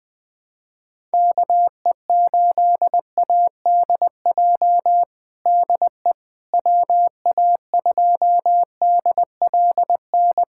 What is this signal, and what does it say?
Morse code, heard as a keyed tone, about 20 wpm, 700 Hz. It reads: KE8ADJ DE WA2DLN
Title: USA Callsign DE Callsign - 6 Characters - 20wpm